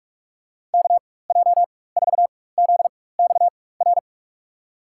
Morse code, heard as a keyed tone, about 42 wpm, spaced as for 20 wpm, 700 Hz. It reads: KJ4ZXR
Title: USA Call Signs - 4x Character Spacing - 45wpm